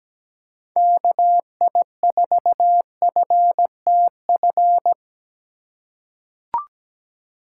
Morse code, heard as a keyed tone, about 17 wpm, 700 Hz. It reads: KI4FTF
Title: USA Call Signs - 6 Characters - 2x Character Spacing - 17wpm